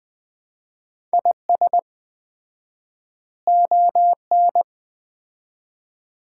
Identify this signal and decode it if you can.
Morse code, heard as a keyed tone, about 20 wpm, 700 Hz. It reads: IS ON
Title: 4x Word-Spacing - Sets of 3 Words - 20wpm